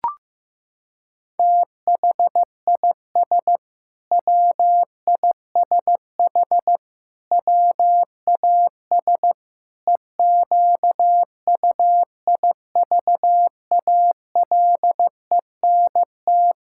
Morse code, heard as a keyed tone, about 15 wpm, 700 Hz. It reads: THIS WISH WAS EQUIVALENT
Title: Sentences from Top 1600 Words 15wpm